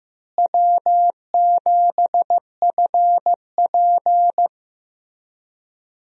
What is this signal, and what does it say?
Morse code, heard as a keyed tone, about 15 wpm, 700 Hz. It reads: W7FP
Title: USA Call Signs - 4 Characters - Repeated 3x - 15wpm